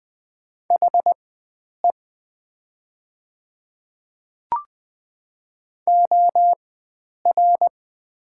Morse code, heard as a keyed tone, about 20 wpm, spaced as for 9 wpm, 700 Hz. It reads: HE OR
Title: 4x Character Spacing - Top 100 Words - 20wpm